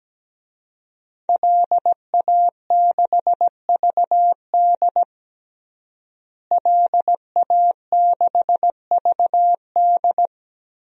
Morse code, heard as a keyed tone, about 17 wpm, 700 Hz. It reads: LA6VD LA6VD
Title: DXCC-100 Call Signs - 5 Characters - Repeated 3x - 17wpm